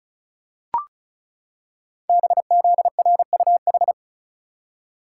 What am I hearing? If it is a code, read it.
Morse code, heard as a keyed tone, about 36 wpm, 690 Hz. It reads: BZRUH
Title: Code Groups - 5 Letters - 35wpm